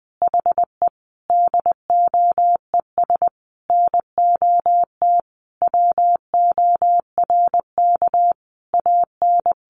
Morse code, heard as a keyed tone, about 20 wpm, 700 Hz. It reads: HE DOES NOT WORK AN
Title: Additional Easy Phrases 20wpm